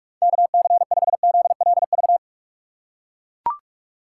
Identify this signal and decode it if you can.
Morse code, heard as a keyed tone, about 41 wpm, 700 Hz. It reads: KC5ZLV
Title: USA Call Signs - 6 Characters - 6x Character Spacing - 45wpm